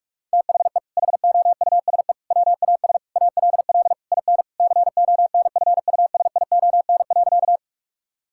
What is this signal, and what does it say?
Morse code, heard as a keyed tone, about 46 wpm, 700 Hz. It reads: THE HOUSE WAS ALL IN CONFUSION.
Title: Sentences from Top 1600 Words 45wpm